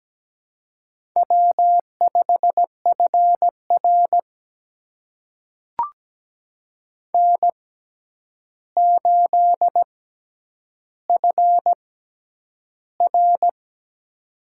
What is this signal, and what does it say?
Morse code, heard as a keyed tone, about 17 wpm, spaced as for 9 wpm, 700 Hz. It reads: W5FR N8FR
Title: USA Call Signs - 4 Characters - 6x Character Spacing - 17wpm